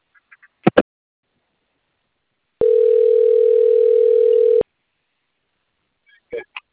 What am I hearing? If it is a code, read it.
Ringback tone (North America), 440 + 480 Hz.
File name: out- - - - - .47786